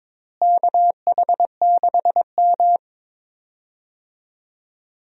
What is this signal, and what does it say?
Morse code, heard as a keyed tone, about 22 wpm, 700 Hz. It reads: KH6M
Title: USA Call Signs - 4 Characters - Repeated 2x - 22wpm